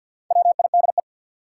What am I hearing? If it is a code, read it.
Morse code, heard as a keyed tone, about 47 wpm, 700 Hz. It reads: WIDE
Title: Sets of 4 Words - New Word Drill - 50wpm